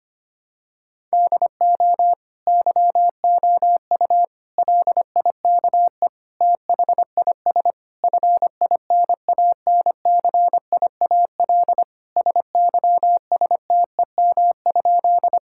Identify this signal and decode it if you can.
Morse code, heard as a keyed tone, about 25 wpm, 700 Hz. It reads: DO YOU LIKE THIS FINANCIAL SYSTEM?
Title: Sentences from Top 1000 Words - Review Part 5 - 25wpm